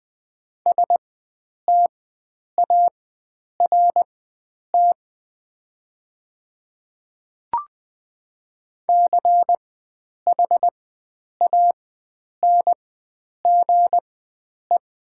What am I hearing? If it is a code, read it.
Morse code, heard as a keyed tone, about 20 wpm, spaced as for 9 wpm, 700 Hz. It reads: START CHANGE
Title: 4x Character Spacing - Top 300 Words - 20wpm